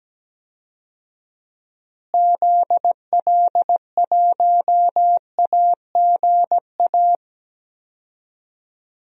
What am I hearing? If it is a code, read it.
Morse code, heard as a keyed tone, about 17 wpm, 700 Hz. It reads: ZL1AGA
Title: DXCC-100 Call Signs - 6 Characters - Repeated 3x - 17wpm